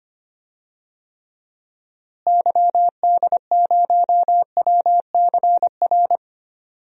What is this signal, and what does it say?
Morse code, heard as a keyed tone, about 25 wpm, 700 Hz. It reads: YD0WCR
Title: DXCC-100 Call Signs - 6 Characters - Repeated 3x - 25wpm